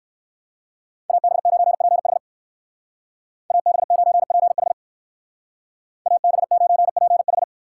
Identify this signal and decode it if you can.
Morse code, heard as a keyed tone, about 59 wpm, 700 Hz. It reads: AB9PH AB9PH AB9PH